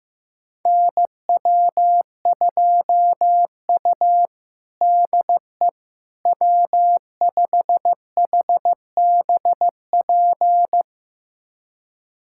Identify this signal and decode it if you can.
Morse code, heard as a keyed tone, about 15 wpm, 700 Hz. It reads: NW2U DE W5HBP
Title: USA Callsign DE Callsign 15wpm